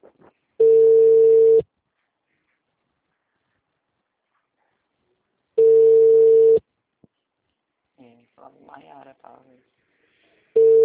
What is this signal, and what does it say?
Ringback tone (China, 450 Hz), with speech.